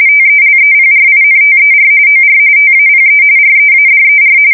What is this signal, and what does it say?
RTTY, about 45.45 baud. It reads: TELEPRINTER OR TAPE READER